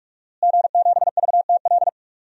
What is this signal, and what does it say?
Morse code, heard as a keyed tone, about 46 wpm, 700 Hz. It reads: G7VTL
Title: DXCC-100 Call Signs - 5 Characters - Repeated 3x - 45wpm